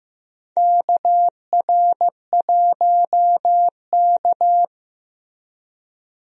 Morse code, heard as a keyed tone, about 15 wpm, 700 Hz. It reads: KR1K
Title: USA Call Signs - 4 Characters - 15wpm